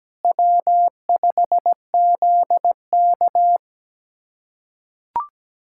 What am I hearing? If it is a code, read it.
Morse code, heard as a keyed tone, about 17 wpm, 700 Hz. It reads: W5ZK
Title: USA Callsign DE Callsign - 4 Characters - 17wpm